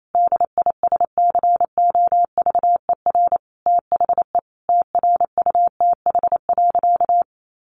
Morse code, heard as a keyed tone, about 28 wpm, 700 Hz. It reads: DISCOVER THE TRUTH.
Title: Sentences from Top 1100 Words 28wpm